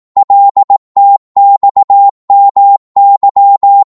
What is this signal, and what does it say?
Morse code, heard as a keyed tone, about 18 wpm, 800 Hz. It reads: LTXMY
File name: Morse / ltr-18wpm